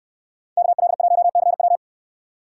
Morse code, heard as a keyed tone, about 66 wpm, 700 Hz. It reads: KL1CW